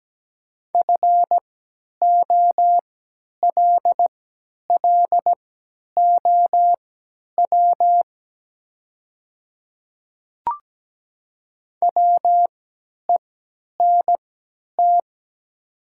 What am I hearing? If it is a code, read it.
Morse code, heard as a keyed tone, about 17 wpm, spaced as for 10 wpm, 700 Hz. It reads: FOLLOW WENT
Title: 3x Character Spacing - Top 200 Words - 17wpm